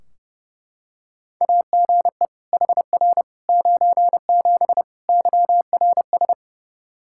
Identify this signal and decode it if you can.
Morse code, heard as a keyed tone, about 29 wpm, 700 Hz. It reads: AGE HR 97 YRS
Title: Practice QSO - Exchange 3 - More Info - 30wpm